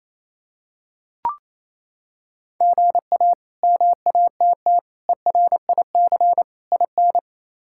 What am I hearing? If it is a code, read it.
Morse code, heard as a keyed tone, about 28 wpm, 700 Hz. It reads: GA MATT ERIC IN
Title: K1USN SST QSO Practice - 28wpm